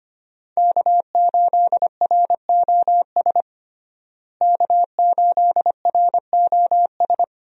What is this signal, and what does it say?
Morse code, heard as a keyed tone, about 25 wpm, 700 Hz. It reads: K8ROS K8ROS
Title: USA Call Signs - 5 Characters - Repeated 3x - 25wpm